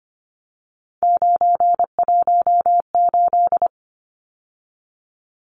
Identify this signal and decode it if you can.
Morse code, heard as a keyed tone, about 25 wpm, 700 Hz. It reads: 918
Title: Numbers - 3-Digit 25wpm